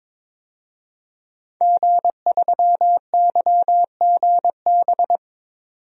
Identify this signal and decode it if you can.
Morse code, heard as a keyed tone, about 22 wpm, 700 Hz. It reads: G3YGB
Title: DXCC-100 Call Signs - 5 Characters - 22wpm